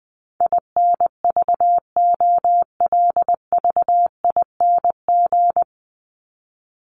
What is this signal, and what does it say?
Morse code, heard as a keyed tone, about 20 wpm, 700 Hz. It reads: INVOLVING
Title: ING Suffix - Top 100 Words with Base Words 20wpm